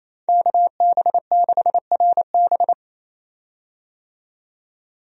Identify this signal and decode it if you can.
Morse code, heard as a keyed tone, about 27 wpm, 700 Hz. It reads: KB6RB
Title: USA Call Signs - 5 Characters - 28wpm